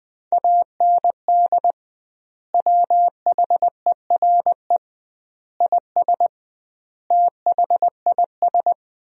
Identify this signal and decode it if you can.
Morse code, heard as a keyed tone, about 20 wpm, 700 Hz. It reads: AND WHERE IS THIS
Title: 2x Word-Spacing - Sentences from Top 200 Words - 20wpm